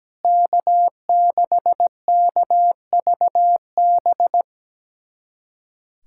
Morse code, heard as a keyed tone, about 17 wpm, 700 Hz. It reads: K6KVB
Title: USA Callsign DE Callsign - 5 Characters - 17wpm